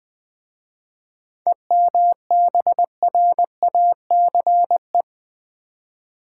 Morse code, heard as a keyed tone, about 20 wpm, 690 Hz. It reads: EMBRACE
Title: 7 Letter Words - First 3 Letters Given - 20wpm